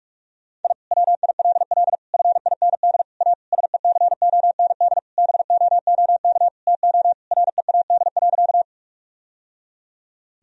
Morse code, heard as a keyed tone, about 46 wpm, 700 Hz. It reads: I WILL FIND A SECOND BOOK TO READ.
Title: Sentences from Top 1000 Words - Review Part 1 - 45wpm